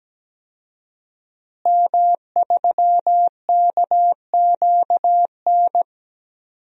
Morse code, heard as a keyed tone, about 17 wpm, 700 Hz. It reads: M3KQN
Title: DXCC-100 Call Signs - 5 Characters - Repeated 3x - 17wpm